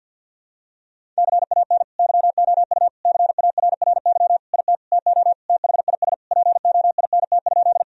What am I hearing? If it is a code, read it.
Morse code, heard as a keyed tone, about 47 wpm, 700 Hz. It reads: CAN YOU CARRY IT TO THIS POINT?
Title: Sentences from Top 1000 Words - Review Part 1 - 50wpm